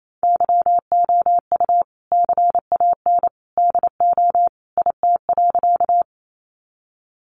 Morse code, heard as a keyed tone, about 28 wpm, 700 Hz. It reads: YOU CAN DO IT.
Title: Sentences from Top 100 Words 28wpm